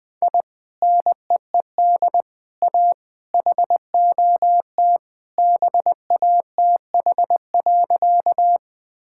Morse code, heard as a keyed tone, about 20 wpm, 700 Hz. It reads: I NEED A HOT BATH.
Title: Sentences from Top 1600 Words 20wpm